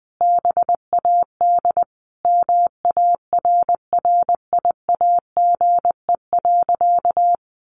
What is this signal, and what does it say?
Morse code, heard as a keyed tone, about 19 wpm, 700 Hz. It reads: BAD MARRIAGE.
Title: Sentences from Top 800 Words 20wpm